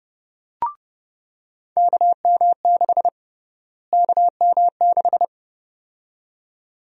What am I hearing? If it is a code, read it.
Morse code, heard as a keyed tone, about 30 wpm, 700 Hz. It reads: KM6 KM6
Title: USA Call Sign Prefixes - Repeated 2x - 30wpm